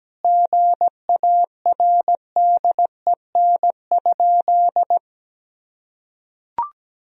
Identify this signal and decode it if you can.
Morse code, heard as a keyed tone, about 17 wpm, 700 Hz. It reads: GARDEN?
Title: Sentences from Top 1000 Words - Review Part 4 - 17wpm